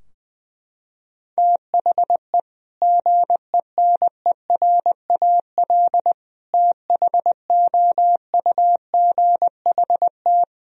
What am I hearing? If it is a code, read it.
Morse code, heard as a keyed tone, about 20 wpm, 700 Hz. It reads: THE GENERAL THOUGHT
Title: Sentences from Top 1000 Words - Review Part 3 - 20wpm